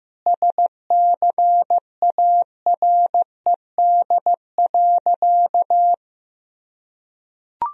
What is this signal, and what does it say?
Morse code, heard as a keyed tone, about 16 wpm, 700 Hz. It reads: SCARED.
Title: Sentences from Top 1000 Words - Review Part 4 - 15wpm